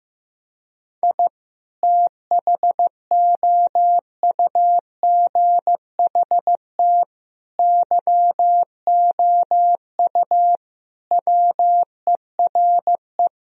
Morse code, heard as a keyed tone, about 15 wpm, 700 Hz. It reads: I THOUGHT YOU WERE
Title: Sentences from Top 1000 Words - Review Part 4 - 15wpm